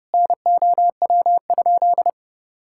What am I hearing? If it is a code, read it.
Morse code, heard as a keyed tone, about 32 wpm, 700 Hz. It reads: NOW?
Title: Sentences from Top 1000 Words - Review Part 5 - 30wpm